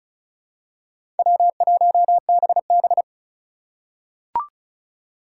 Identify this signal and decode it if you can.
Morse code, heard as a keyed tone, about 36 wpm, 700 Hz. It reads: W1BB
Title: USA Call Signs - 4 Characters - 3x Character Spacing - 35wpm